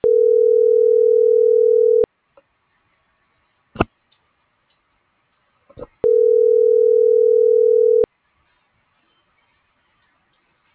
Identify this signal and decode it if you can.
Ringback tone (North America), 440 + 480 Hz.
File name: CENTRAL SOTOVA / out carlos.sotova